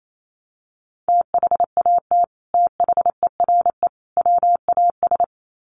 Morse code, heard as a keyed tone, about 28 wpm, 700 Hz. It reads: THAT THERE WAS
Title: Sets of 3 Words 28wpm